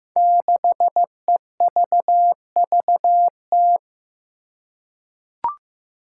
Morse code, heard as a keyed tone, about 15 wpm, 700 Hz. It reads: 6EVVT
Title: Five Challenging Letter-Number Combinations 15wpm